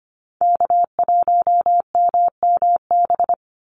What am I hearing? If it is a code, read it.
Morse code, heard as a keyed tone, about 25 wpm, 700 Hz. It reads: K1MMB
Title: Call Signs - 25wpm